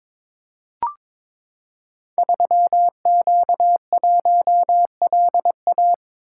Morse code, heard as a keyed tone, about 23 wpm, 700 Hz. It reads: 3Q1LA
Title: Code Groups - 5 Letter:Numbers - 22wpm